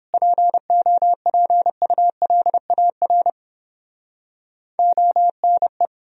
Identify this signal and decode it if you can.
Morse code, heard as a keyed tone, about 31 wpm, 700 Hz. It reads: POPULAR ONE